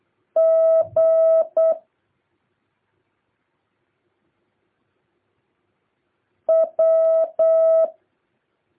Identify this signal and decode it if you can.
Morse code, heard as a keyed tone, about 8 wpm, 650 Hz. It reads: G W
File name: jh8cbh / 2023cw